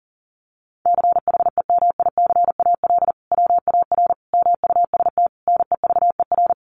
Morse code, heard as a keyed tone, about 40 wpm, 700 Hz. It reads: CHEMICAL WAR MUST NEVER
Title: Sentences from Top 1200 Words 40wpm